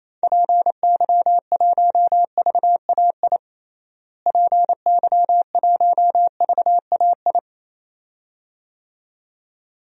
Morse code, heard as a keyed tone, about 28 wpm, 700 Hz. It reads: PY1VAI PY1VAI
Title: DXCC-100 Call Signs - 6 Characters - Repeated 2x - 28wpm